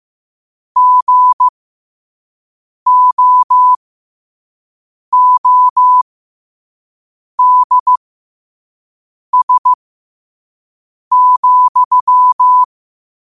Morse code, heard as a keyed tone, about 15 wpm, spaced as for 5 wpm, 1000 Hz. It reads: GOODS,
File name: qotdfiles / 5wpm